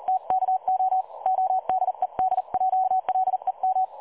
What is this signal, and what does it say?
Morse code, heard as a keyed tone, about 36 wpm, 750 Hz, with radio noise. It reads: TKO ODEDJ LEM
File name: dlazaawansowanych / nagraniazpasma / websdr recording 3565.0kHz